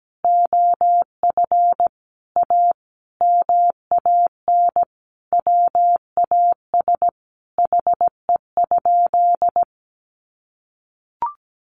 Morse code, heard as a keyed tone, about 17 wpm, 700 Hz. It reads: OF A MAN WAS HE?
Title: Sentences from Top 1300 Words 17wpm